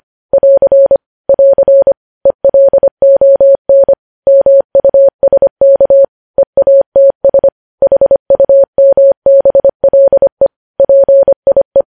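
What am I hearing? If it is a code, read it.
Morse code, heard as a keyed tone, about 25 wpm, 550 Hz. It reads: <AR> <AR> ELON MUSK EATS HUMBLE PIE
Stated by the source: News headlines sent in Morse code at 25 wpm.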